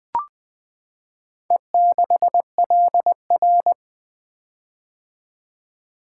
Morse code, heard as a keyed tone, about 20 wpm, 700 Hz. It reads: E6LR
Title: Four Letter-Number Combinations - 20wpm